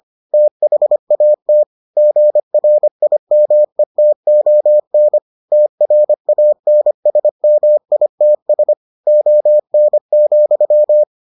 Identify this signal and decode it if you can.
Morse code, heard as a keyed tone, about 25 wpm, 600 Hz. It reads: THAT GRIMETON TRANSMITS ON,